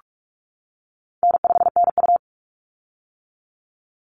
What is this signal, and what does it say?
Morse code, heard as a keyed tone, about 46 wpm, 700 Hz. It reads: N5NU
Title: Easy Call Signs - 45wpm